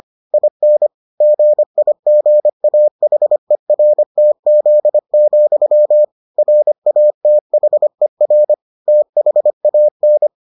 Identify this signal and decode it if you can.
Morse code, heard as a keyed tone, about 25 wpm, 600 Hz. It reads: IN GIGAHERTZ, RATHER THAN